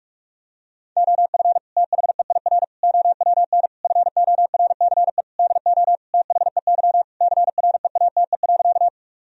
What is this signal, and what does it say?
Morse code, heard as a keyed tone, about 41 wpm, 700 Hz. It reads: OF THEIR OWN FORCE DO THEY CREATE.
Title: Sentences from Top 1000 Words - Review Part 3 - 45wpm